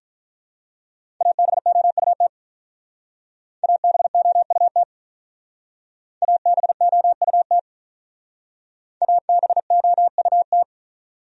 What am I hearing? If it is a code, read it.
Morse code, heard as a keyed tone, about 44 wpm, 700 Hz. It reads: ABOUT ABOUT ABOUT ABOUT